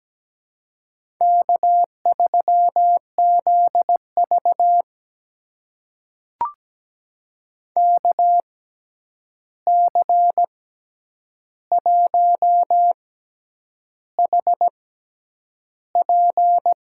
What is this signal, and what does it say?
Morse code, heard as a keyed tone, about 17 wpm, spaced as for 8 wpm, 700 Hz. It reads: K3ZV KC1HP
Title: USA Call Signs - 6x Character Spacing - 17wpm